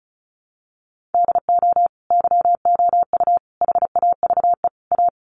Morse code, heard as a keyed tone, about 34 wpm, 700 Hz. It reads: DO YOU HAVE A
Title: Sentences from Top 1400 Words 35wpm